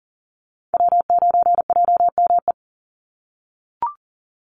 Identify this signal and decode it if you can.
Morse code, heard as a keyed tone, about 40 wpm, 700 Hz. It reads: W9JME
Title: USA Call Signs 40wpm